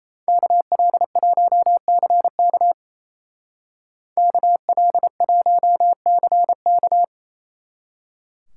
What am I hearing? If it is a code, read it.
Morse code, heard as a keyed tone, about 34 wpm, 700 Hz. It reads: KL1CK KL1CK
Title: Speed Racing - USA 2x2 Call Signs 28wpm